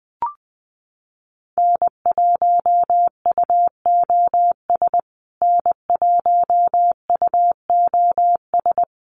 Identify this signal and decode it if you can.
Morse code, heard as a keyed tone, about 20 wpm, 700 Hz. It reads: N1UOS N1UOS
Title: USA Call Signs Repeated 3x 20wpm